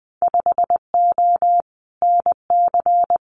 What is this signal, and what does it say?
Morse code, heard as a keyed tone, about 20 wpm, 700 Hz. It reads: 5O NC
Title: Winter Field Day Exchanges 20wpm